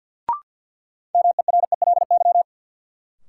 Morse code, heard as a keyed tone, about 53 wpm, 700 Hz. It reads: MERELY
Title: Sentences from Top 1400 Words - New Word Drill - 50wpm